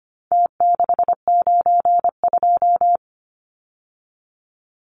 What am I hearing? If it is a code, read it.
Morse code, heard as a keyed tone, about 25 wpm, 700 Hz. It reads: T692
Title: Numbers - Contest Style - 4-Digit 25wpm